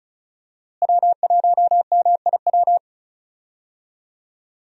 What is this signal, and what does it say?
Morse code, heard as a keyed tone, about 34 wpm, 700 Hz. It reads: W1MIW
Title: USA Call Signs - 5 Characters - Repeated 3x - 35wpm